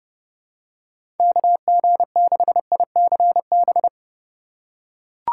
Morse code, heard as a keyed tone, about 32 wpm, 700 Hz. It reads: KG6ICB
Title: USA Call Signs - 3x Character Spacing - 30wpm